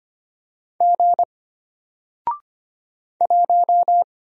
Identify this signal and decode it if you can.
Morse code, heard as a keyed tone, about 25 wpm, 700 Hz. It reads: G 1
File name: Single-Letter-Number-25wpm